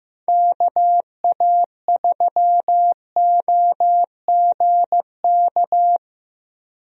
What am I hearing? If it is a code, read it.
Morse code, heard as a keyed tone, about 15 wpm, 700 Hz. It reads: KA3OGK
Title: USA Call Signs - 6 Characters - Repeated 3x - 15wpm